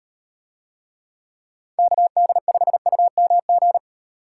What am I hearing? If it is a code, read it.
Morse code, heard as a keyed tone, about 37 wpm, 700 Hz. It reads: KD5UMG